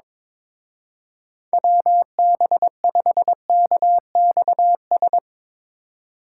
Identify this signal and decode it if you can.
Morse code, heard as a keyed tone, about 22 wpm, 700 Hz. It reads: WB5KXS
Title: Speed Racing - USA Call Signs 22wpm